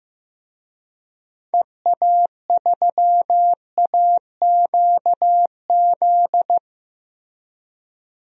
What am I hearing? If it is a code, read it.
Morse code, heard as a keyed tone, about 15 wpm, 700 Hz. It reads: EA3AQZ
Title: DXCC-100 Call Signs - 6 Characters - Repeated 2x - 15wpm